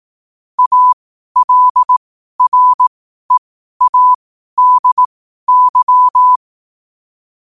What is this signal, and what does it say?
Morse code, heard as a keyed tone, about 18 wpm, spaced as for 12 wpm, 1000 Hz. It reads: ALREADY
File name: qotdfiles / 12wpm